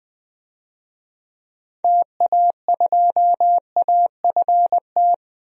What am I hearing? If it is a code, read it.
Morse code, heard as a keyed tone, about 20 wpm, 700 Hz. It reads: TA2AFT
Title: DXCC-100 Call Signs - 6 Characters - 20wpm